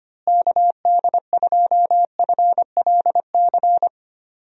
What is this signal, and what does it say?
Morse code, heard as a keyed tone, about 25 wpm, 700 Hz. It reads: KD2FLC
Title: USA Call Signs - 6 Characters - Repeated 2x - 25wpm